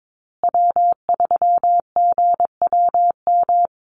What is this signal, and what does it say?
Morse code, heard as a keyed tone, about 22 wpm, 700 Hz. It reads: W3GWM
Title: Call Signs Repeated 2x - 22wpm